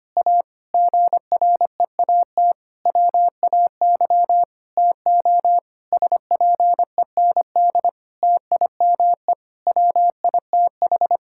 Morse code, heard as a keyed tone, about 25 wpm, 700 Hz. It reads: A GREAT WAY TO SPEND TIME WITH
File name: practice / 25wpm / Difficult Sentences with Hints - 25wpm